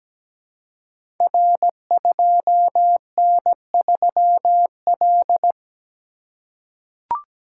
Morse code, heard as a keyed tone, about 17 wpm, 700 Hz. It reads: R2N3L
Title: Code Groups - 5 Letter:Numbers - 17wpm